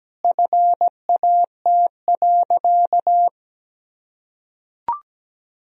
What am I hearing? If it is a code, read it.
Morse code, heard as a keyed tone, about 17 wpm, 700 Hz. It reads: FAT.
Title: Sentences from Top 1000 Words - Review Part 4 - 17wpm